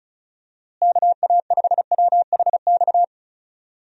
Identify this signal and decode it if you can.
Morse code, heard as a keyed tone, about 34 wpm, 700 Hz. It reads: KA5WHX
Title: USA Call Signs - 6 Characters - Repeated 3x - 35wpm